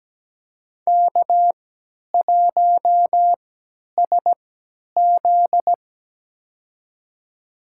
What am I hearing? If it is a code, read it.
Morse code, heard as a keyed tone, about 17 wpm, spaced as for 10 wpm, 700 Hz. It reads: K1SZ
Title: USA Call Signs - 4 Characters - 3x Character Spacing - 17wpm